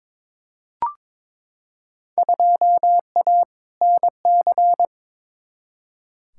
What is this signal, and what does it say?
Morse code, heard as a keyed tone, about 21 wpm, 700 Hz. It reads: 2A NC
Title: Field Day Exchanges Encore - 22wpm